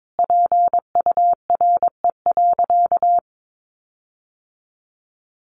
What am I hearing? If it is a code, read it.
Morse code, heard as a keyed tone, about 22 wpm, 700 Hz. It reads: PURE.
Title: Sentences from Top 1500 Words 22wpm